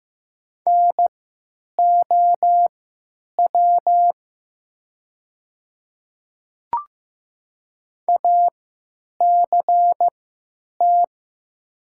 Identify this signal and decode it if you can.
Morse code, heard as a keyed tone, about 15 wpm, spaced as for 9 wpm, 700 Hz. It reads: NOW ACT
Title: 3x Character Spacing - Top 200 Words - 15wpm